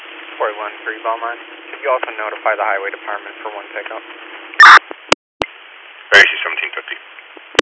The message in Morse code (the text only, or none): none